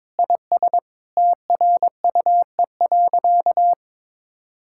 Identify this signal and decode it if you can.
Morse code, heard as a keyed tone, about 22 wpm, 700 Hz. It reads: IS TRUE.
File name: practice / 22wpm / Sentences with 8-Word Phrases - 22wpm